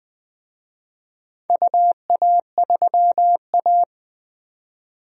Morse code, heard as a keyed tone, about 20 wpm, 700 Hz. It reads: UA3A
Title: DXCC-100 Call Signs - 4 Characters - 20wpm